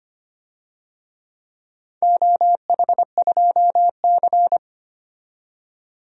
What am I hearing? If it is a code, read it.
Morse code, heard as a keyed tone, about 25 wpm, 700 Hz. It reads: OH2C
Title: DXCC-100 Call Signs - 4 Characters - 25wpm